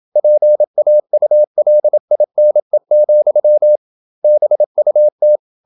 Morse code, heard as a keyed tone, about 27 wpm, 600 Hz. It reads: PAULINE, BUT
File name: Never 0027wpm